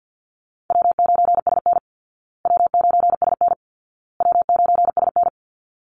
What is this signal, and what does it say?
Morse code, heard as a keyed tone, about 50 wpm, 700 Hz. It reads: W9SN W9SN W9SN
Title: Easy Call Signs Repeated 3x - 50wpm